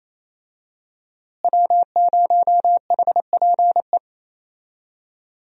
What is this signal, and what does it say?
Morse code, heard as a keyed tone, about 28 wpm, 700 Hz. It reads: W0HPE
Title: USA Call Signs - 5 Characters - Repeated 3x - 28wpm